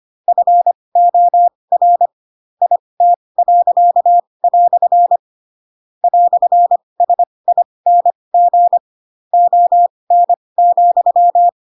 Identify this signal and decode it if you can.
Morse code, heard as a keyed tone, about 25 wpm, spaced as for 20 wpm, 700 Hz. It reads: FOR IT." "SING ON,
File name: Pinocchio / Pinocchio ch4 15 25 ik1hge